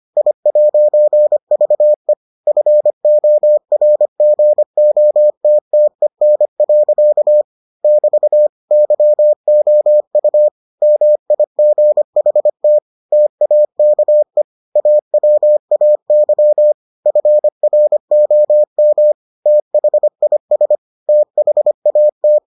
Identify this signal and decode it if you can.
Morse code, heard as a keyed tone, about 25 wpm, 600 Hz. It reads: I'VE FORGOTTEN. <BT> YOU MIGHT TAKE AWAY FROM THIS THAT